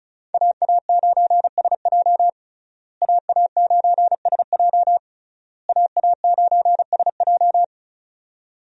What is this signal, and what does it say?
Morse code, heard as a keyed tone, about 36 wpm, 700 Hz. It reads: AA9SJ AA9SJ AA9SJ
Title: USA Call Signs - 5 Characters - Repeated 3x - 35wpm